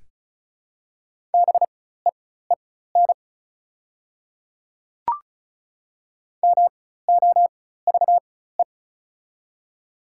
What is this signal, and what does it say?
Morse code, heard as a keyed tone, about 34 wpm, spaced as for 16 wpm, 700 Hz. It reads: BEEN MOVE
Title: 4x Character Spacing - Top 300 Words - 35wpm